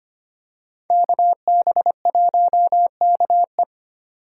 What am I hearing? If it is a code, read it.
Morse code, heard as a keyed tone, about 25 wpm, 700 Hz. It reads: KB1KE
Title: USA Call Signs - 5 Characters - 4x Character Spacing - 25wpm